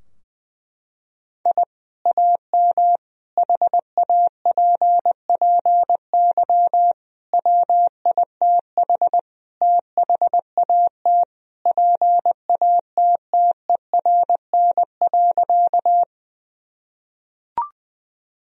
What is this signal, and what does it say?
Morse code, heard as a keyed tone, about 21 wpm, 700 Hz. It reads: I AM HAPPY WITH THAT PATTERN.
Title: Sentences from Top 1000 Words - Review Part 2 - 20wpm